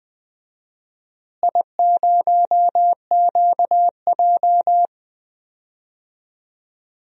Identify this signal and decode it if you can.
Morse code, heard as a keyed tone, about 20 wpm, 700 Hz. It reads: I0QJ
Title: DXCC-100 Call Signs - 4 Characters - Repeated 2x - 20wpm